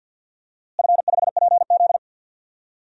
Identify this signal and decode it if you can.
Morse code, heard as a keyed tone, about 52 wpm, 700 Hz. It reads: F5PZ
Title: DXCC-100 Call Signs - 4 Characters - Repeated 3x - 50wpm